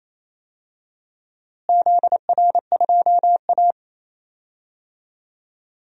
Morse code, heard as a keyed tone, about 28 wpm, 700 Hz. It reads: ZR2A
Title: DXCC-100 Call Signs - 4 Characters - Repeated 2x - 28wpm